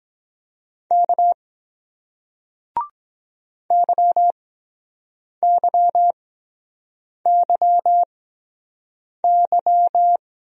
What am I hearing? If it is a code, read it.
Morse code, heard as a keyed tone, about 23 wpm, spaced as for 7 wpm, 700 Hz. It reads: K YYYY